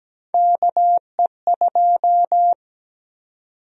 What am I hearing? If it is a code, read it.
Morse code, heard as a keyed tone, about 17 wpm, 700 Hz. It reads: KE2
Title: USA Call Sign Prefixes - 17wpm